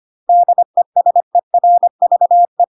Morse code, heard as a keyed tone, about 25 wpm, 700 Hz. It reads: DESERVE
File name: Pinocchio / Pinocchio ch3 25 25 ik1hge